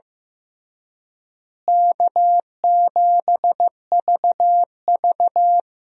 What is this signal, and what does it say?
Morse code, heard as a keyed tone, about 15 wpm, 700 Hz. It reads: K7VV
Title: USA Call Signs - 4 Characters - Repeated 3x - 15wpm